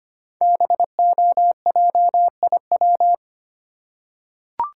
Morse code, heard as a keyed tone, about 25 wpm, 700 Hz. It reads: BOJIW
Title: Code Groups - 5 Letters - 25wpm